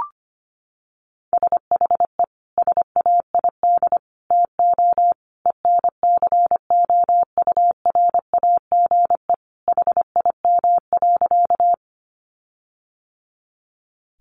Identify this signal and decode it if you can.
Morse code, heard as a keyed tone, about 25 wpm, 700 Hz. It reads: SHE SAID TO ENCOURAGE HIM.
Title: Sentences from Top 1400 Words 25wpm